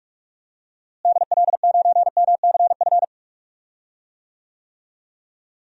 Morse code, heard as a keyed tone, about 44 wpm, 700 Hz. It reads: DL9MCF
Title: DXCC-100 Call Signs - 6 Characters - Repeated 3x - 45wpm